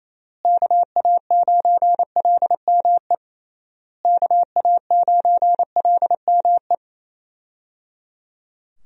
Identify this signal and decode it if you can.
Morse code, heard as a keyed tone, about 28 wpm, 700 Hz. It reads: KA9LME KA9LME
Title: USA Call Signs - 6 Characters - Repeated 3x - 28wpm